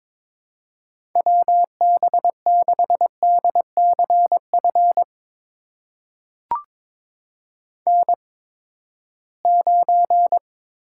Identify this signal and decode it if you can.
Morse code, heard as a keyed tone, about 22 wpm, spaced as for 15 wpm, 700 Hz. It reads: WB6DCF N9
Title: USA Call Signs - 8x Character Spacing - 22wpm